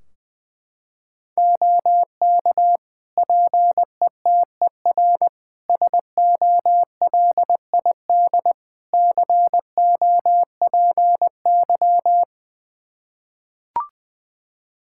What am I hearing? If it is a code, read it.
Morse code, heard as a keyed tone, about 21 wpm, 700 Hz. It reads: OK PETER SOLID COPY
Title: K1USN SST QSO Practice 20wpm